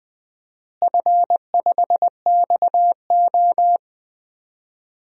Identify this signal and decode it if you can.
Morse code, heard as a keyed tone, about 21 wpm, 700 Hz. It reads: F5XO
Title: DXCC-100 Call Signs - 4 Characters - Repeated 2x - 20wpm